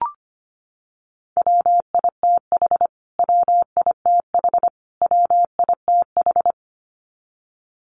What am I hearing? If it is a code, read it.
Morse code, heard as a keyed tone, about 25 wpm, 700 Hz. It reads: WITH WITH WITH
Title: Top 200 Words Repeated 25wpm